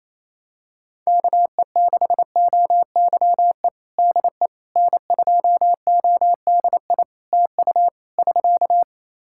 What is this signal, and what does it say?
Morse code, heard as a keyed tone, about 28 wpm, 700 Hz. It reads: KE6OYE DE N2ODI TU <SK>
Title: K1USN SST QSO Practice 28wpm